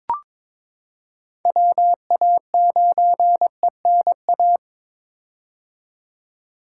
Morse code, heard as a keyed tone, about 22 wpm, 700 Hz. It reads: WA9ENA
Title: K1USN SST QSO Practice - 22wpm